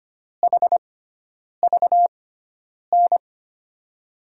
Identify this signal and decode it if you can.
Morse code, heard as a keyed tone, about 25 wpm, spaced as for 9 wpm, 700 Hz. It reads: HVN
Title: USA Call Signs - 5 Characters - 6x Character Spacing - 25wpm